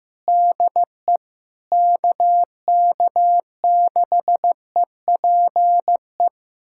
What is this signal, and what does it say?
Morse code, heard as a keyed tone, about 15 wpm, 700 Hz. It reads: DE KK6EPE
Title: USA Callsign DE Callsign 15wpm